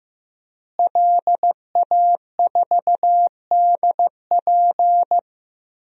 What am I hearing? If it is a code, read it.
Morse code, heard as a keyed tone, about 15 wpm, 700 Hz. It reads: LA4DP
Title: DXCC-100 Call Signs - 5 Characters - Repeated 2x - 15wpm